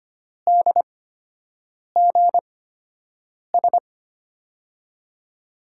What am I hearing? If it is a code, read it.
Morse code, heard as a keyed tone, about 25 wpm, spaced as for 7 wpm, 700 Hz. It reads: DGS
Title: USA Call Signs - 6 Characters - 8x Character Spacing - 25wpm